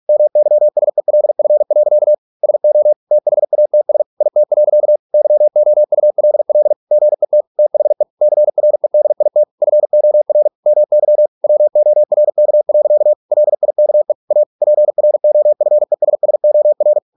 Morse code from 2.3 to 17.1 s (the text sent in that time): SO THATS IT. YOULL GET THE CREDIT FOR MY WORK. LIKE A PROFESSOR